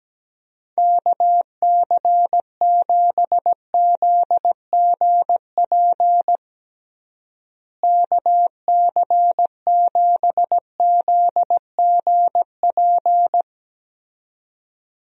Morse code, heard as a keyed tone, about 17 wpm, 700 Hz. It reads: KC7ZGP KC7ZGP
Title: USA Call Signs - 6 Characters - Repeated 2x - 17wpm